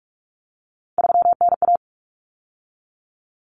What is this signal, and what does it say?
Morse code, heard as a keyed tone, about 44 wpm, 700 Hz. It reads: 3NA
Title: Common Cut Numbers - 3-Digit 45wpm